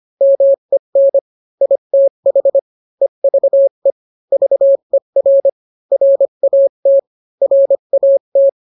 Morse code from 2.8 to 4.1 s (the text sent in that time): EVE